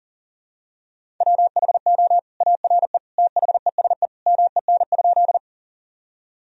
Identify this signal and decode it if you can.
Morse code, heard as a keyed tone, about 40 wpm, 700 Hz. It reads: WHO ARE THESE MEN?
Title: Sentences from Top 1000 Words - Review Part 1 - 40wpm